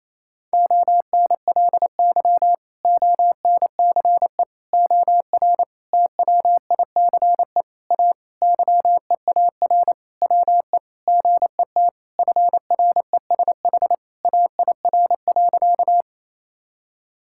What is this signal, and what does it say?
Morse code, heard as a keyed tone, about 27 wpm, 700 Hz. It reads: ONLY ONCE OR TWICE A YEAR WE GET FRESH AIR.
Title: Sentences from Top 1600 Words 28wpm